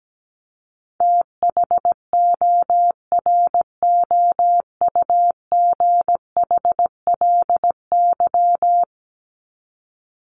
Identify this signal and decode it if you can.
Morse code, heard as a keyed tone, about 17 wpm, 700 Hz. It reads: THOROUGHLY
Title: LY Suffix - Top 100 Words with Base Words 17wpm